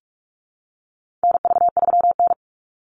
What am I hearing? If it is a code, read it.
Morse code, heard as a keyed tone, about 44 wpm, 700 Hz. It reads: NV3N
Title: Call Signs Repeated 2x - 45wpm